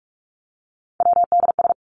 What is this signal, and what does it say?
Morse code, heard as a keyed tone, about 44 wpm, 700 Hz. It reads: WDS
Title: Abbreviations Repeated 3x - 45wpm